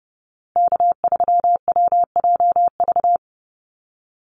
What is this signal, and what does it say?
Morse code, heard as a keyed tone, about 30 wpm, 700 Hz. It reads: K3WJV
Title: Call Signs Repeated 3x - 30wpm